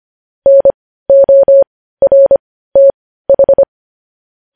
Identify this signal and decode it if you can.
Morse code, heard as a keyed tone, about 25 wpm, spaced as for 15 wpm, 550 Hz. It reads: NORTH
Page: News Headlines in Morse Code at 15 WPM